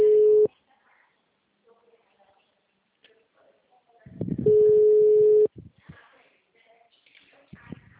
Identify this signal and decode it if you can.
Ringback tone (Europe, 425 Hz), with speech.
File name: out- -2046- - - .4275